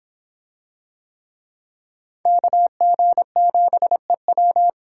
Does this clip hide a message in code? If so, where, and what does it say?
Yes: Morse code from 2.3 to 4.7 s, heard as a keyed tone, about 26 wpm, 700 Hz. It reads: KG7EW